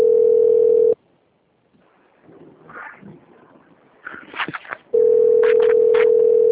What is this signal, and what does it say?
Ringback tone (North America), 440 + 480 Hz.
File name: rg-1600-unknown- - - .20631